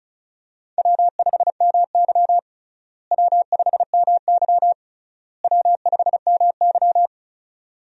Morse code, heard as a keyed tone, about 36 wpm, 700 Hz. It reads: W5MY W5MY W5MY
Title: USA Call Signs - 4 Characters - Repeated 3x - 35wpm